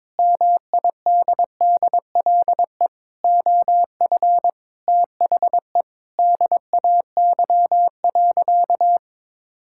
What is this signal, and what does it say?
Morse code, heard as a keyed tone, about 22 wpm, 700 Hz. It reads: MIDDLE OF THE DAY.
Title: Sentences from Top 1000 Words - Review Part 6 - 22wpm